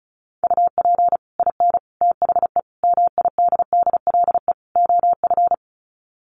Morse code, heard as a keyed tone, about 34 wpm, 700 Hz. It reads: UP IN THE MIDDLE OF
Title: Sets of 5 Words Encore 35wpm